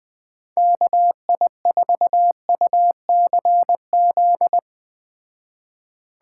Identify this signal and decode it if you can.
Morse code, heard as a keyed tone, about 20 wpm, 700 Hz. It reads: KI4UCZ